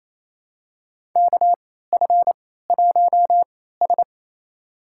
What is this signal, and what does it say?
Morse code, heard as a keyed tone, about 28 wpm, spaced as for 16 wpm, 700 Hz. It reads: KF1S
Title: USA Call Signs - 4 Characters - 3x Character Spacing - 28wpm